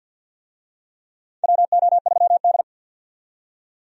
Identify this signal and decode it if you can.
Morse code, heard as a keyed tone, about 52 wpm, 700 Hz. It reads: WO3D
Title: USA Call Signs - 4 Characters - 50wpm